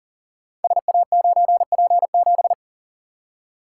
Morse code, heard as a keyed tone, about 42 wpm, 690 Hz. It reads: SA9P7
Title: Code Groups - 5 Letter:Numbers - 40wpm